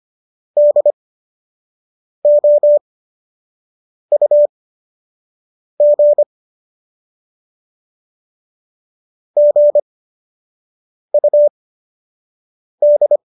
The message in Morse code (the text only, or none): DOUG GUD